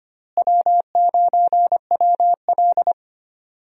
Morse code, heard as a keyed tone, about 25 wpm, 700 Hz. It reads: W9WL
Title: USA Call Signs - 4 Characters - Repeated 3x - 25wpm